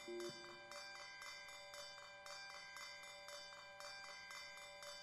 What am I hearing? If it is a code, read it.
Emergency Alert System (SAME) tones, heard as harsh buzzing bursts. It is heard from 0.8 to 4.7 s.